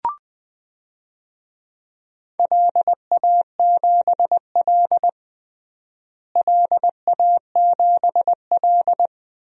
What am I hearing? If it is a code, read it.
Morse code, heard as a keyed tone, about 20 wpm, 700 Hz. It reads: LA7L LA7L
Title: DXCC-100 Call Signs - 4 Characters - Repeated 2x - 20wpm